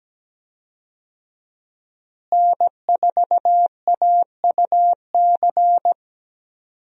Morse code, heard as a keyed tone, about 17 wpm, 700 Hz. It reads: N4AUC
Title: USA Call Signs - 5 Characters - Repeated 2x - 17wpm